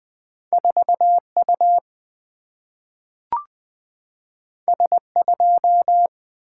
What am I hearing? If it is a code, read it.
Morse code, heard as a keyed tone, about 20 wpm, 700 Hz. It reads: 4U S2
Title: Single-Letter Single-Number 20wpm